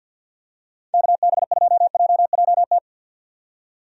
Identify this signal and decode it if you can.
Morse code, heard as a keyed tone, about 51 wpm, 700 Hz. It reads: KB2JJT
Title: USA Call Signs - 6 Characters - 50wpm